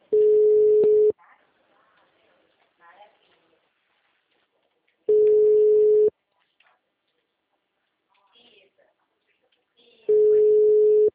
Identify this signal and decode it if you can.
Ringback tone (Europe, 425 Hz), with speech.